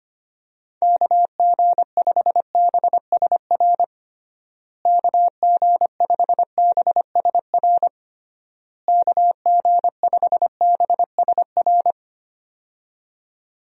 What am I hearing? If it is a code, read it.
Morse code, heard as a keyed tone, about 25 wpm, 700 Hz. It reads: KG5BSR KG5BSR KG5BSR
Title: USA Call Signs - 6 Characters - Repeated 3x - 25wpm